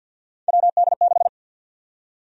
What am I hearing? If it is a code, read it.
Morse code, heard as a keyed tone, about 52 wpm, 700 Hz. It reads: WD6
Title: USA Call Sign Prefixes - 50wpm